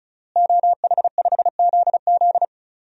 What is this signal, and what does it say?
Morse code, heard as a keyed tone, about 34 wpm, 700 Hz. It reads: OH5ZZ
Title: DXCC-100 Call Signs - 5 Characters - Repeated 3x - 35wpm